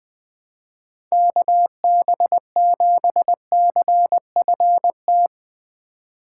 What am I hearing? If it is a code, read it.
Morse code, heard as a keyed tone, about 20 wpm, 700 Hz. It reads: KB7CFT
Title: Speed Racing - USA 2x3 Call Signs 17wpm